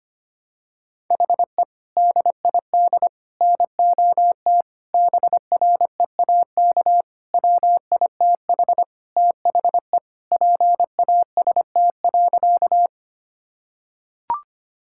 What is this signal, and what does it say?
Morse code, heard as a keyed tone, about 25 wpm, 700 Hz. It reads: HE DID NOT BREAK WITH THE PAST.
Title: Sentences from Top 1000 Words - Review Part 6 - 25wpm